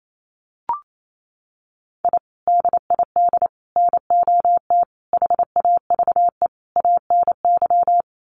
Morse code, heard as a keyed tone, about 28 wpm, 700 Hz. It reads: I DID NOT HAVE ANY
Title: Sets of 5 Words Encore 28wpm